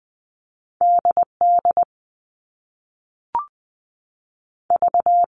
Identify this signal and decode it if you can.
Morse code, heard as a keyed tone, about 20 wpm, 700 Hz. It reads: D D V
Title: All Cut Numbers - 2-Digit 20wpm